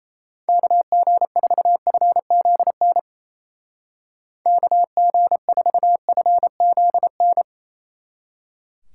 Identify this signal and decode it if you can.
Morse code, heard as a keyed tone, about 32 wpm, 700 Hz. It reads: KG4FZN KG4FZN
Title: Speed Racing - USA 2x3 Call Signs 28wpm